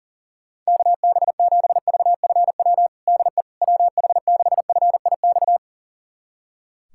Morse code, heard as a keyed tone, about 40 wpm, 700 Hz. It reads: KB7VFW DE WH6FIX
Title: USA Callsign DE Callsign - 6 Characters - 40wpm